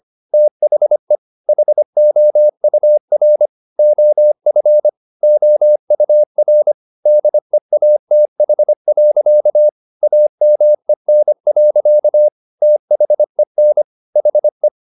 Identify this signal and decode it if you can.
Morse code, heard as a keyed tone, about 25 wpm, 600 Hz. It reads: THE HOUR OF OUR DEATH. AMEN. THEN HE
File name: the old man and the sea part 1 25wpm